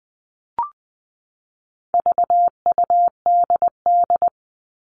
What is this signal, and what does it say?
Morse code, heard as a keyed tone, about 20 wpm, 700 Hz. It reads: VUDD
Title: All Cut Numbers - 4-Digit 20wpm